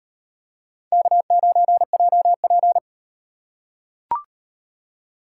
Morse code, heard as a keyed tone, about 35 wpm, 700 Hz. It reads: K9JP